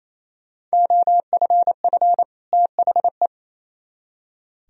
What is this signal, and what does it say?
Morse code, heard as a keyed tone, about 36 wpm, 700 Hz. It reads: OFF THE